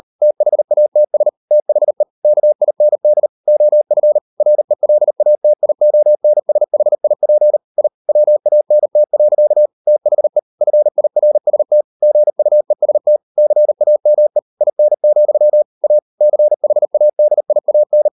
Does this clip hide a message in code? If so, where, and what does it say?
Yes: Morse code from 0.2 to 18.1 s, heard as a keyed tone, about 40 wpm, 600 Hz. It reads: THATS THE KIND OF RELATIONSHIP I WANT. THE FIRST GUEST CAME IN, A CHADIAN